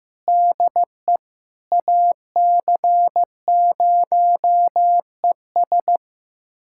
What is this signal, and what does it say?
Morse code, heard as a keyed tone, about 15 wpm, 700 Hz. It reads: DE AC0ES
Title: USA Callsign DE Callsign 15wpm